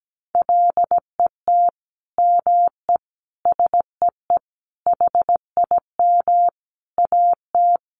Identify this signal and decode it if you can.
Morse code, heard as a keyed tone, about 17 wpm, 700 Hz. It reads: LET ME SEE HIM AT
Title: Sentences from Top 300 Words 17wpm